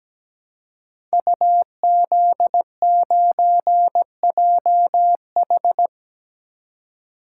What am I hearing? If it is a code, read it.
Morse code, heard as a keyed tone, about 17 wpm, 700 Hz. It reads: UZ9JH
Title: Code Groups - 5 Letter:Numbers - 17wpm